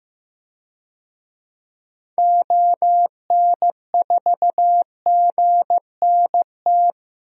Morse code, heard as a keyed tone, about 15 wpm, 700 Hz. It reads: ON4GNT
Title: DXCC-100 Call Signs - 6 Characters - 15wpm